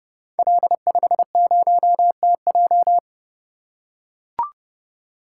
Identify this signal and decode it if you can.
Morse code, heard as a keyed tone, about 30 wpm, 700 Hz. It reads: L50TJ
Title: Code Groups - 5 Letter:Numbers - 30wpm